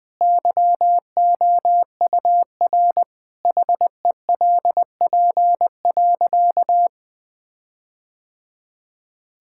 Morse code, heard as a keyed tone, about 21 wpm, 700 Hz. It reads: YOUR HELP.
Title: Sentences with 4-Word Phrases - Spoken Beginning - 20wpm